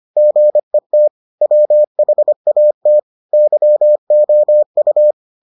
Morse code, heard as a keyed tone, about 25 wpm, 600 Hz. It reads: GET WHAT YOU